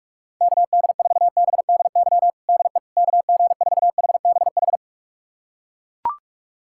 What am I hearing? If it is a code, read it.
Morse code, heard as a keyed tone, about 41 wpm, 700 Hz. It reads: KD4BDY DE KG4HBH
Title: USA Callsign DE Callsign 45wpm